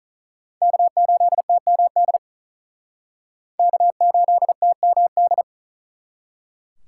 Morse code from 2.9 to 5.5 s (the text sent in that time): K8TMD